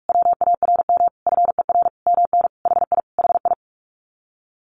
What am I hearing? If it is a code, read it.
Morse code, heard as a keyed tone, about 44 wpm, 700 Hz. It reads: WARM FER MN HI HI
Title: Common QSO Phrases Encore 45wpm